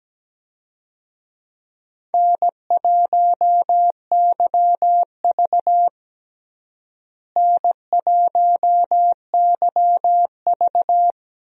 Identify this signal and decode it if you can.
Morse code, heard as a keyed tone, about 17 wpm, 700 Hz. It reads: N1YV N1YV
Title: USA Call Signs - 4 Characters - Repeated 2x - 17wpm